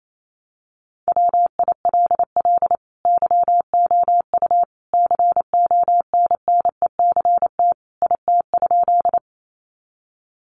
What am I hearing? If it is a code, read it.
Morse code, heard as a keyed tone, about 28 wpm, 700 Hz. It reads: WILL YOU CONNECT IT?
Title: Sentences from Top 1500 Words 28wpm